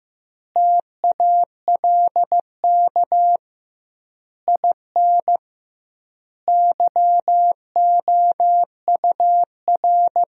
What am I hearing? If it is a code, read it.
Morse code, heard as a keyed tone, about 15 wpm, 700 Hz. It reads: TALK IN YOUR
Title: 2x Word-Spacing - Sentences from Top 500 Words - 15wpm